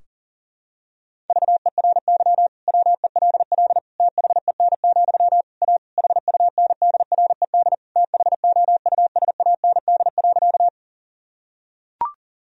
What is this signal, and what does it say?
Morse code, heard as a keyed tone, about 40 wpm, 700 Hz. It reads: VERY WELL THEN, A HUNDRED THOUSAND.
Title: Sentences from Top 1000 Words - Review Part 3 - 40wpm